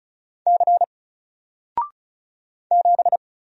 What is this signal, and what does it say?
Morse code, heard as a keyed tone, about 36 wpm, 700 Hz. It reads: C 7
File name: Single-Letter-Number-35wpm